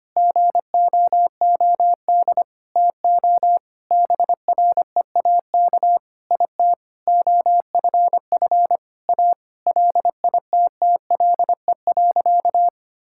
Morse code, heard as a keyed tone, about 25 wpm, 700 Hz. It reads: GOOD TO BREAK IT OFF A LITTLE.
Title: Sentences from Top 1000 Words - Review Part 6 - 25wpm